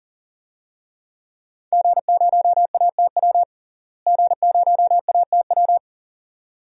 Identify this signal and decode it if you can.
Morse code, heard as a keyed tone, about 40 wpm, 700 Hz. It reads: G0ATW G0ATW
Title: DXCC-100 Call Signs - 5 Characters - Repeated 2x - 40wpm